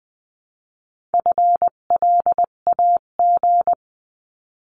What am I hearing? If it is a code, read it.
Morse code, heard as a keyed tone, about 20 wpm, 700 Hz. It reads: FLAG
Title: 258 Lesson 42c - Words 16-20wpm